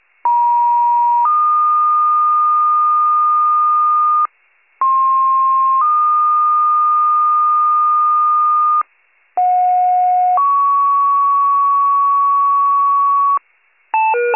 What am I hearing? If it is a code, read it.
Two-tone page, heard as a one-second tone, then a three-second tone, about 950 then 1220 Hz, about 1060 then 1220 Hz, about 730 then 1120 Hz. It is heard from 0.2 to 13.4 s.
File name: pager/98 RESCUE